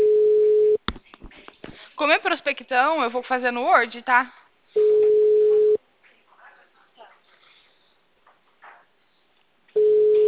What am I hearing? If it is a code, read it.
Ringback tone (Europe, 425 Hz), with speech.